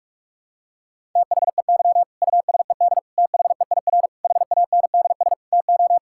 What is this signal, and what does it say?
Morse code, heard as a keyed tone, about 41 wpm, 700 Hz. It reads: THEY USED THEIR HANDS TO
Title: Sentences from Top 1000 Words - Review Part 6 - 45wpm